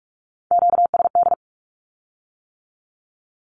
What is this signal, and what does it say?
Morse code, heard as a keyed tone, about 46 wpm, 700 Hz. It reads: QSD
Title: Q-Codes with Spoken Meaning - 45wpm